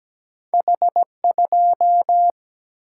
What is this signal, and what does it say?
Morse code, heard as a keyed tone, about 17 wpm, 700 Hz. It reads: H2
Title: Single-Letter Single-Number 17wpm